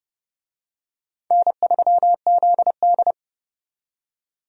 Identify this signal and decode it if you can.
Morse code, heard as a keyed tone, about 31 wpm, 700 Hz. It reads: N3ZD
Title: USA Call Signs - 4 Characters - 30wpm